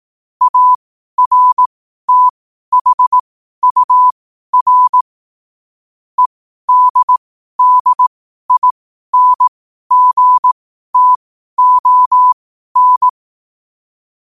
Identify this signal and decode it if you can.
Morse code, heard as a keyed tone, about 18 wpm, spaced as for 12 wpm, 1000 Hz. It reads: ARTHUR EDDINGTON
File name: qotdfiles / 12wpm